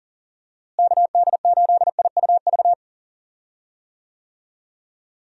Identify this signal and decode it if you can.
Morse code, heard as a keyed tone, about 40 wpm, 700 Hz. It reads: KD8IUV
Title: USA Call Signs - 6 Characters - Repeated 2x - 40wpm